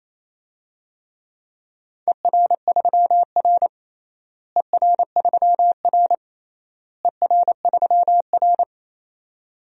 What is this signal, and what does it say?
Morse code, heard as a keyed tone, about 28 wpm, 700 Hz. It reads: ER3R ER3R ER3R
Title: DXCC-100 Call Signs - 4 Characters - Repeated 3x - 28wpm